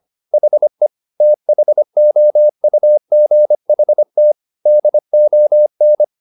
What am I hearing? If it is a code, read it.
Morse code, heard as a keyed tone, about 25 wpm, 600 Hz. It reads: HE THOUGHT DON